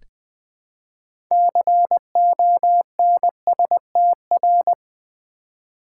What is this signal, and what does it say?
Morse code, heard as a keyed tone, about 20 wpm, 700 Hz. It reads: CONSTR
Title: Build A Word - 3 Letters At A Time - 9-Letter Words - 20wpm